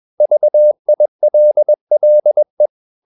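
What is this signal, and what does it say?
Morse code, heard as a keyed tone, about 21 wpm, 600 Hz. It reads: VILLE
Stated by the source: Morse practice at 21 wpm, slow-speed holiday exercise.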